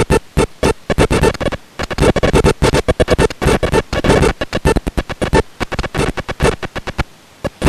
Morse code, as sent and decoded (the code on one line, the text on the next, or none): none